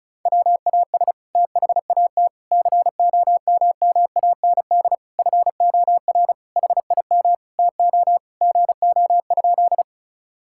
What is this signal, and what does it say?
Morse code, heard as a keyed tone, about 34 wpm, 700 Hz. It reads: WAS THAT COMMAND FOR HIM TO GO?
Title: Sentences from Top 1600 Words 35wpm